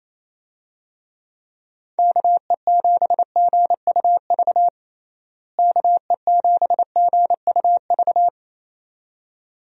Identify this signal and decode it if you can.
Morse code, heard as a keyed tone, about 28 wpm, 700 Hz. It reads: KE7GUV KE7GUV
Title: USA Call Signs - 6 Characters - Repeated 2x - 28wpm